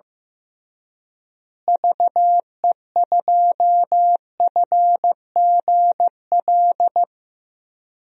Morse code, heard as a keyed tone, about 15 wpm, 700 Hz. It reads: VE2FGL
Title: DXCC-100 Call Signs - 6 Characters - Repeated 3x - 15wpm